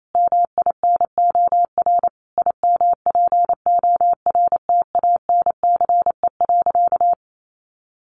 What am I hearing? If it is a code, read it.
Morse code, heard as a keyed tone, about 28 wpm, 700 Hz. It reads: MINOR IMPORTANCE.
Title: Sentences from Top 1500 Words 28wpm